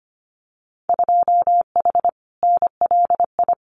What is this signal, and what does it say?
Morse code, heard as a keyed tone, about 25 wpm, 700 Hz. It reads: 2H NLI
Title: Winter Field Day Exchanges Encore 25wpm